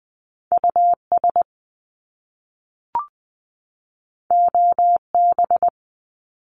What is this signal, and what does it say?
Morse code, heard as a keyed tone, about 20 wpm, 700 Hz. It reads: U S O B
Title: Top Two Letter Combinations in Words 20wpm